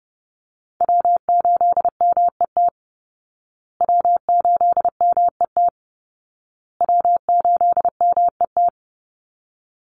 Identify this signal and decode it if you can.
Morse code, heard as a keyed tone, about 30 wpm, 700 Hz. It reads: W8MET W8MET W8MET
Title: Call Signs Repeated 2x - 30wpm